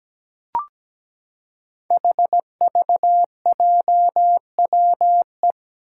Morse code, heard as a keyed tone, about 17 wpm, 700 Hz. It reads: HVJWE
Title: Five Challenging Letter-Number Combinations 17wpm